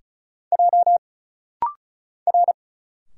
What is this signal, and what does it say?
Morse code, heard as a keyed tone, about 34 wpm, 700 Hz. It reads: J R
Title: Single Letters - Rapid-Fire 35wpm